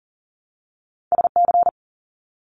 Morse code, heard as a keyed tone, about 40 wpm, 700 Hz. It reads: SC
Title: US States Repeated 3x 40wpm